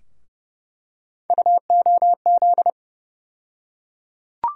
Morse code, heard as a keyed tone, about 30 wpm, 700 Hz. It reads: UOZ
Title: USA Call Sign Postfixes - 3 Letters 30wpm